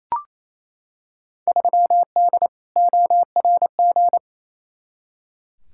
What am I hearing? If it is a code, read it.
Morse code, heard as a keyed tone, about 28 wpm, 700 Hz. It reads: 3D ORG
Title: Field Day Exchanges Encore - 28wpm